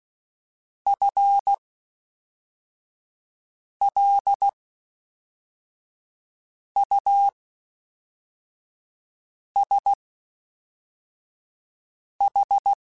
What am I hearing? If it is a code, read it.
Morse code, heard as a keyed tone, about 16 wpm, spaced as for 4 wpm, 800 Hz. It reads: FLUSH
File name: Spaced / FL Runs sp mp3s / FL Runs sp @5